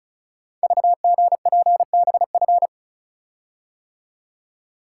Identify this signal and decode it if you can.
Morse code, heard as a keyed tone, about 35 wpm, 700 Hz. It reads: VGPBF
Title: Code Groups - 5 Letters - 35wpm